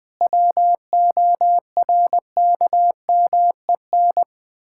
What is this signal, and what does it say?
Morse code, heard as a keyed tone, about 20 wpm, 700 Hz. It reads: WORKMEN
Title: Pangram Challenge v2 - 20wpm